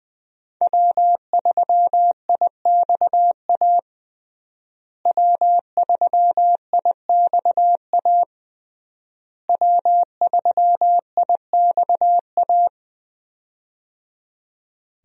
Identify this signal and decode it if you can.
Morse code, heard as a keyed tone, about 21 wpm, 700 Hz. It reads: W3IXA W3IXA W3IXA
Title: USA Call Signs - 5 Characters - Repeated 3x - 20wpm